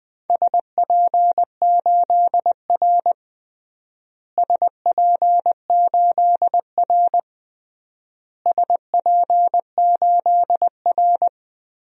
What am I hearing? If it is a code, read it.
Morse code, heard as a keyed tone, about 20 wpm, 700 Hz. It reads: SP8R SP8R SP8R
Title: DXCC-100 Call Signs - 4 Characters - Repeated 3x - 20wpm